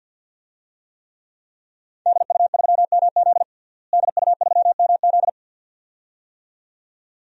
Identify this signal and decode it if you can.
Morse code, heard as a keyed tone, about 52 wpm, 700 Hz. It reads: DU3MZ DU3MZ
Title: DXCC-100 Call Signs - 5 Characters - Repeated 2x - 50wpm